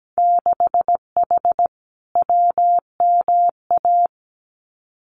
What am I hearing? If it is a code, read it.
Morse code, heard as a keyed tone, about 17 wpm, 700 Hz. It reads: 6H WMA
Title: Winter Field Day Exchanges Encore 17wpm